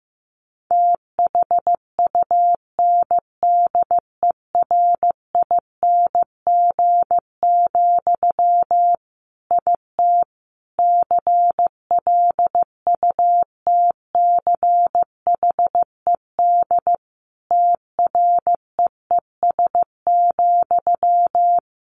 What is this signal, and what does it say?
Morse code, heard as a keyed tone, about 15 wpm, 700 Hz. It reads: THUNDERING, IT CLUTCHED TREES,
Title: A Sound Of Thunder 15wpm